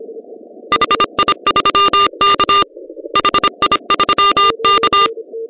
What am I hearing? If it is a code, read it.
Morse code, heard as a keyed tone, about 25 wpm, 1200 Hz. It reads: HI3K HI3K
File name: 2015 cq ww cw / 2015 cqwww k5tr - 0-4